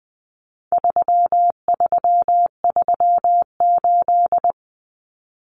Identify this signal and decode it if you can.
Morse code, heard as a keyed tone, about 20 wpm, 700 Hz. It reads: 3338
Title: Common Cut Numbers - 4-Digit 20wpm